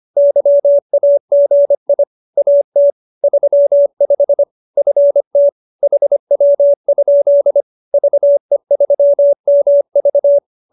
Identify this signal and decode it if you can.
Morse code, heard as a keyed tone, about 25 wpm, 570 Hz. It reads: YAGI AT 35 FT HW? VE3MV
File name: QSO303 25